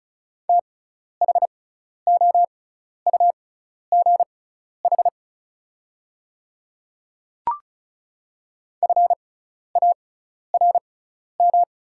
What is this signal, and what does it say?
Morse code, heard as a keyed tone, about 34 wpm, spaced as for 12 wpm, 700 Hz. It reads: THOUGH FARM
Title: 6x Character Spacing - Top 500 Words - 35wpm